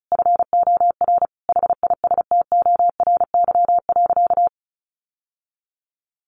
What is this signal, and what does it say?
Morse code, heard as a keyed tone, about 34 wpm, 700 Hz. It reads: FOR HISTORY.
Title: Sentences from Top 600 Words 35wpm